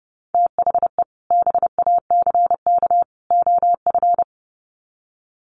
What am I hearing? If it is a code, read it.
Morse code, heard as a keyed tone, about 30 wpm, 700 Hz. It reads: THE BACK OF
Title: Sets of 3 Words 30wpm